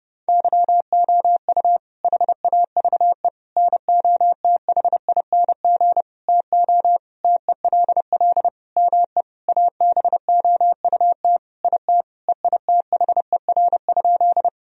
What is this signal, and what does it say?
Morse code, heard as a keyed tone, about 32 wpm, 700 Hz. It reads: YOU HAVE NOTHING TO TELL ME ABOUT IT EITHER?
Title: Sentences from Top 1000 Words - Review Part 4 - 30wpm